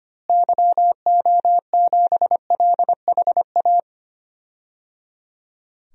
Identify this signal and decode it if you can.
Morse code, heard as a keyed tone, about 25 wpm, 700 Hz. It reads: YO7LHA
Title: DXCC-100 Call Signs - 6 Characters - Repeated 3x - 25wpm